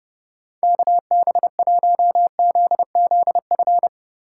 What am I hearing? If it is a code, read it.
Morse code, heard as a keyed tone, about 30 wpm, 700 Hz. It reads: KB1ZZF
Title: USA Call Signs - 6 Characters - 6x Character Spacing - 30wpm